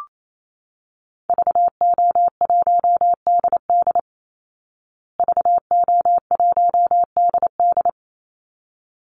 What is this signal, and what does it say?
Morse code, heard as a keyed tone, about 28 wpm, 700 Hz. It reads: VO1DD VO1DD
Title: Call Signs Repeated 3x - 28wpm